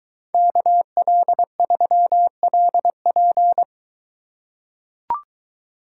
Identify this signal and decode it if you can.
Morse code, heard as a keyed tone, about 23 wpm, 700 Hz. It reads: KL3LP